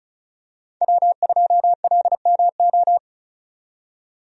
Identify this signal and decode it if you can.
Morse code, heard as a keyed tone, about 36 wpm, 700 Hz. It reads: W2LMO
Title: USA Call Signs - 5 Characters - 35wpm